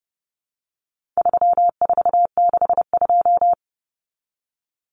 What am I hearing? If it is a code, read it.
Morse code, heard as a keyed tone, about 30 wpm, 700 Hz. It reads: 3462
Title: Common Cut Numbers - 4-Digit 30wpm